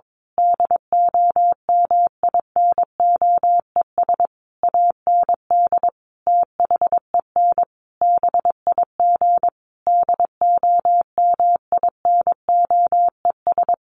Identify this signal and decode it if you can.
Morse code, heard as a keyed tone, about 22 wpm, 700 Hz. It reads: DOMINOES AND THEN BIG DOMINOES
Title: A Sound Of Thunder 22wpm